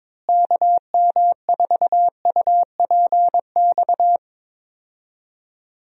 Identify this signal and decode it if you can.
Morse code, heard as a keyed tone, about 22 wpm, 700 Hz. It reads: KM4UPX
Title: USA Call Signs - 6 Characters - Repeated 2x - 22wpm